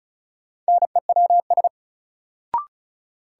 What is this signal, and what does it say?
Morse code, heard as a keyed tone, about 36 wpm, 700 Hz. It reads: NEWS
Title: Sentences from Top 800 Words - New Word Drill - 35wpm